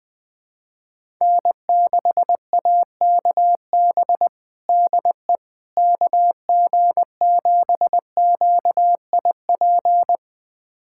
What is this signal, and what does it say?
Morse code, heard as a keyed tone, about 20 wpm, 700 Hz. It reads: N6AKB DE KG7QIP
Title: USA Callsign DE Callsign 20wpm